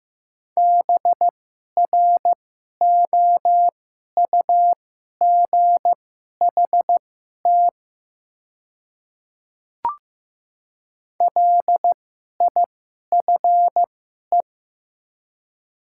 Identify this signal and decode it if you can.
Morse code, heard as a keyed tone, about 15 wpm, spaced as for 11 wpm, 700 Hz. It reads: BROUGHT LIFE
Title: 2x Character Spacing - Top 500 Words - 15wpm